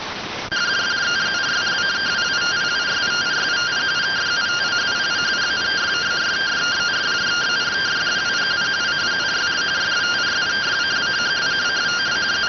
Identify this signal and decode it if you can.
Two-tone teleprinter signal, about 170 Hz shift.